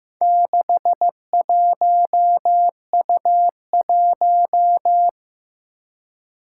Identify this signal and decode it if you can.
Morse code, heard as a keyed tone, about 15 wpm, 700 Hz. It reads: 61U1
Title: Four Challenging Letter-Number Combinations 15wpm